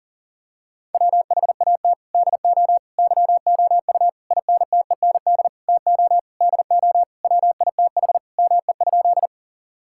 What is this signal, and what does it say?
Morse code, heard as a keyed tone, about 37 wpm, 700 Hz. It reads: WHAT DO YOU INTEND TO DO WITH ME?
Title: Sentences from Top 1600 Words 40wpm